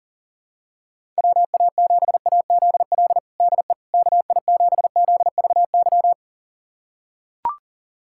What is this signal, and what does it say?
Morse code, heard as a keyed tone, about 40 wpm, 700 Hz. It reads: WA7AZL DE KI7ZVY
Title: USA Callsign DE Callsign - 6 Characters - 40wpm